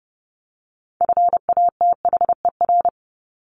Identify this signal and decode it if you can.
Morse code, heard as a keyed tone, about 30 wpm, 700 Hz. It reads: FATHER
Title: Top 200 Words Repeated 30wpm